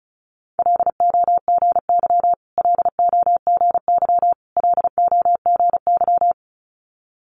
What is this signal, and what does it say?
Morse code, heard as a keyed tone, about 36 wpm, 700 Hz. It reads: LOGY LOGY LOGY
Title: Multi-letter Phonemes Repeated 3x 35wpm